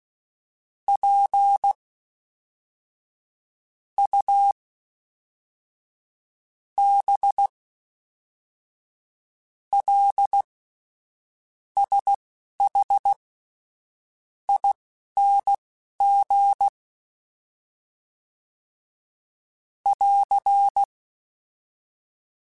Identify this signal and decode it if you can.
Morse code, heard as a keyed tone, about 16 wpm, spaced as for 5 wpm, 800 Hz. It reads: PUBLSHING <AR>
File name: Special INGs / Special INGs-mp3s / Fast ing @10